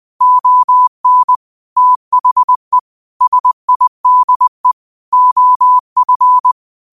Morse code, heard as a keyed tone, about 20 wpm, 1000 Hz. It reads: ON THE SIDE OF
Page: Daily quotes in Morse Code at 20 Words per Minute. Use this to improve your code copying skill.